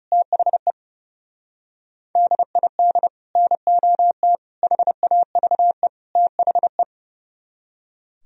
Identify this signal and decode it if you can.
Morse code, heard as a keyed tone, about 37 wpm, 700 Hz. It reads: THE DID NOT HAVE THE
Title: Speed-Racing - Sets of 4 Words 30wpm